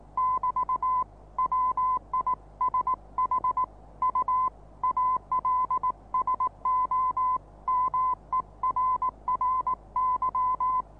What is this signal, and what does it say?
Morse code, heard as a keyed tone, about 19 wpm, 1030 Hz. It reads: <BT> WISH U ALSO MERRY
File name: repos / Morsekurs / DJ4UF Morsen26